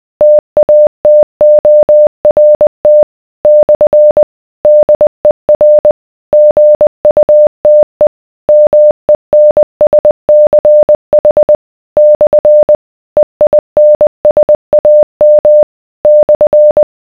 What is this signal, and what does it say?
Morse code, heard as a keyed tone, about 20 wpm, 600 Hz. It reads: TATORT / DER GUTE MENSCH / EINSAM /